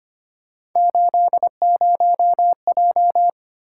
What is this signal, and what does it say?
Morse code, heard as a keyed tone, about 25 wpm, 700 Hz. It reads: 80J
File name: Three Letter-Number Combinations - 25wpm